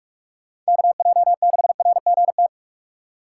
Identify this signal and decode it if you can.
Morse code, heard as a keyed tone, about 59 wpm, spaced as for 52 wpm, 700 Hz. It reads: KJ6RGT
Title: Speed Racing - USA 2x3 Call Signs 45wpm